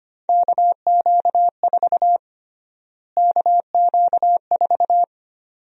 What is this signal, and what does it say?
Morse code, heard as a keyed tone, about 25 wpm, 700 Hz. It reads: KQ4 KQ4
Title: USA Call Sign Prefixes - Repeated 2x - 25wpm